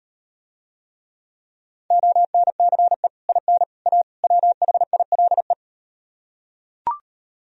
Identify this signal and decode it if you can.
Morse code, heard as a keyed tone, about 38 wpm, 700 Hz. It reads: ONCE IN A WHILE